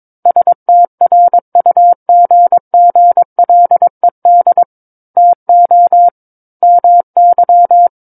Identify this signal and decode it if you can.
Morse code, heard as a keyed tone, about 22 wpm, 700 Hz. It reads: STRUGGLED TO MY
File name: Book One / 13-Chapter 13 - 22 WPM